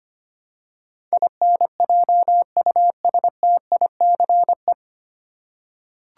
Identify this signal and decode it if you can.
Morse code, heard as a keyed tone, about 25 wpm, 690 Hz. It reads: INJUSTICE
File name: practice / 25wpm / 9 Letter Words - First 3 Letters Given - 25wpm